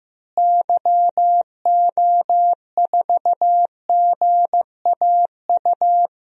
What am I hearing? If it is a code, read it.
Morse code, heard as a keyed tone, about 15 wpm, 700 Hz. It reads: YO4GAU
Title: DXCC-100 Call Signs - 6 Characters - Repeated 3x - 15wpm